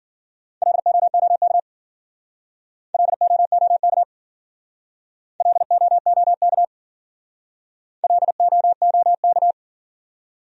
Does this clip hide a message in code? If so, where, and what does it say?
Yes: Morse code from 0.6 to 9.5 s, heard as a keyed tone, about 52 wpm, 700 Hz. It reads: LOOK LOOK LOOK LOOK